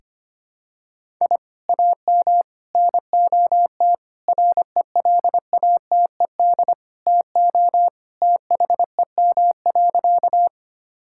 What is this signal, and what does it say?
Morse code, heard as a keyed tone, about 25 wpm, 700 Hz. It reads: I AM NOT RELATED TO THEM.
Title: Sentences from Top 1000 Words - Review Part 5 - 25wpm